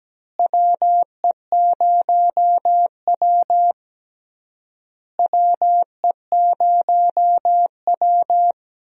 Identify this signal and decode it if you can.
Morse code, heard as a keyed tone, about 17 wpm, 700 Hz. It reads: WE0W WE0W
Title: USA Call Signs - 4 Characters - Repeated 3x - 17wpm